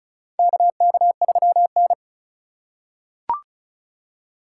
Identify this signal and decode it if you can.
Morse code, heard as a keyed tone, about 36 wpm, 700 Hz. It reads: KK3N
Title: USA Call Signs - 4 Characters - 3x Character Spacing - 35wpm